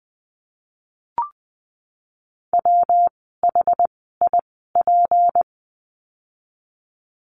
Morse code, heard as a keyed tone, about 20 wpm, spaced as for 14 wpm, 700 Hz. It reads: WHIP
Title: 153 Lesson 25f - Letter p - Words 20wpm